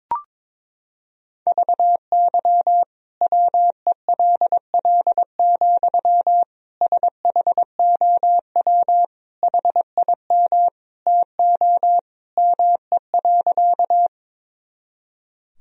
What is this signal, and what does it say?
Morse code, heard as a keyed tone, about 22 wpm, 700 Hz. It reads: VY WELL, SHOW HIM TO ME.
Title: Sentences from Top 200 Words and Abbreviations 22wpm